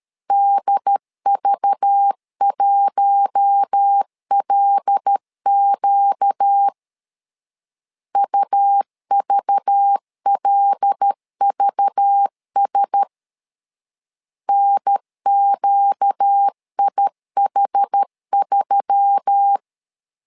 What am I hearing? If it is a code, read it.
Morse code, heard as a keyed tone, about 13 wpm, 800 Hz. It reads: DV1LQ UVLVS NQIH3
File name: LEZIONI / 5 LBVC / 5 LBV 12wpm 1855